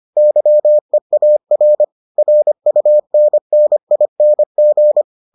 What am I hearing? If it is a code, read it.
Morse code, heard as a keyed tone, about 25 wpm, 600 Hz. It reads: YEAR RUNNING